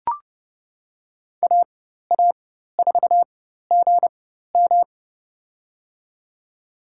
Morse code, heard as a keyed tone, about 32 wpm, spaced as for 14 wpm, 700 Hz. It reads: AA4GM
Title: USA Call Signs - 5 Characters - 4x Character Spacing - 30wpm